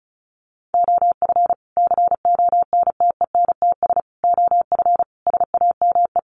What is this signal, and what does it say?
Morse code, heard as a keyed tone, about 34 wpm, 700 Hz. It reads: OF CONTENTS OF SAME
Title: Sets of 5 Words Encore2 35wpm